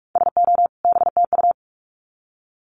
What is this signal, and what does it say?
Morse code, heard as a keyed tone, about 44 wpm, 700 Hz. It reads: SO BTU
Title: Common QSO Phrases 45wpm